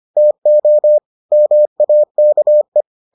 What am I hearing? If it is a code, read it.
Morse code, heard as a keyed tone, about 25 wpm, 600 Hz. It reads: TO MAKE